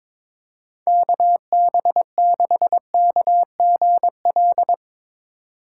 Morse code, heard as a keyed tone, about 22 wpm, 700 Hz. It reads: KB6KGL
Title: USA Call Signs - 6 Characters - 6x Character Spacing - 22wpm